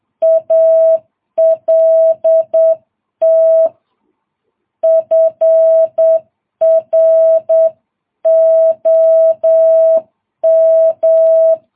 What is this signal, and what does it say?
Morse code, heard as a keyed tone, about 8 wpm, 650 Hz. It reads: ALT FROM
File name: jh8cbh